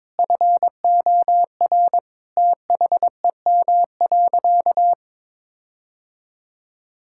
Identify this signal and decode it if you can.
Morse code, heard as a keyed tone, about 22 wpm, 700 Hz. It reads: FOR THEM.
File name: Sentences with 7-Word Phrases - 22wpm